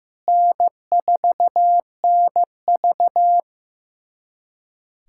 Morse code, heard as a keyed tone, about 15 wpm, 700 Hz. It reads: N4NV
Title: USA Callsign DE Callsign - 4 Characters - 15wpm